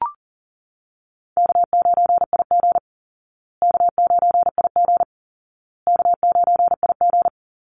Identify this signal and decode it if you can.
Morse code, heard as a keyed tone, about 40 wpm, 700 Hz. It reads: K9IG K9IG K9IG
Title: Call Signs Repeated 2x - 40wpm